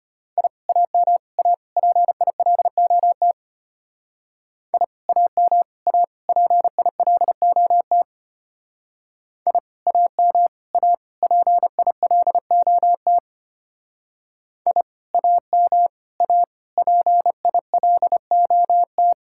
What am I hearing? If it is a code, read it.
Morse code, heard as a keyed tone, about 32 wpm, 700 Hz. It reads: I AM A PILOT I AM A PILOT I AM A PILOT I AM A PILOT